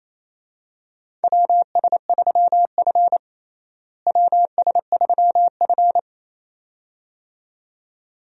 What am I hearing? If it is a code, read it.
Morse code, heard as a keyed tone, about 28 wpm, 700 Hz. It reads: WS3F WS3F
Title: USA Call Signs - 4 Characters - Repeated 2x - 28wpm